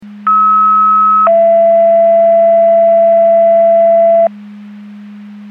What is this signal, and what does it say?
Two-tone page, heard as a one-second tone, then a three-second tone, about 1290 then 670 Hz. It is heard from 0.3 to 4.3 s.
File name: Portage 154.385 Paging